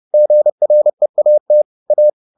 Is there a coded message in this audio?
Yes: Morse code from 0.1 to 2.1 s, heard as a keyed tone, about 30 wpm, 600 Hz. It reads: GREAT A